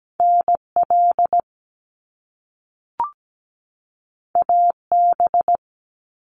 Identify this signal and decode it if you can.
Morse code, heard as a keyed tone, about 17 wpm, 700 Hz. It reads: NL AB
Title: Canadian Provinces and Territories 17wpm